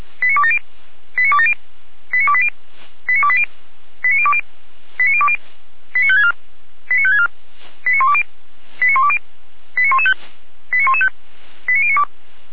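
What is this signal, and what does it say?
Five-tone selective call (ZVEI), heard as quick runs of five tones, tones of about 70 ms. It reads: 89270 89270 89280 89280 89020 89020 89564 89564 89120 89120 89115 89115 89092
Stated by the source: ZVEI five-tone call received on an SDR.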